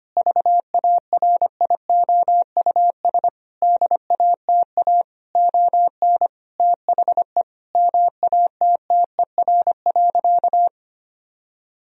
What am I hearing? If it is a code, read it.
Morse code, heard as a keyed tone, about 25 wpm, 700 Hz. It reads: VARIOUS DATA ON THE MATTER.
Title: Sentences from Top 1000 Words - Review Part 4 - 25wpm